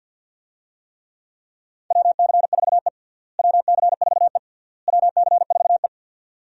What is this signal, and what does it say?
Morse code, heard as a keyed tone, about 52 wpm, 700 Hz. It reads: WC4E WC4E WC4E
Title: USA Call Signs - 4 Characters - Repeated 3x - 50wpm